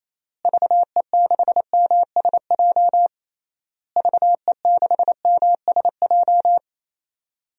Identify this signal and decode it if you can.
Morse code, heard as a keyed tone, about 28 wpm, 700 Hz. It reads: VE6MSJ VE6MSJ
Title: DXCC-100 Call Signs - 6 Characters - Repeated 3x - 28wpm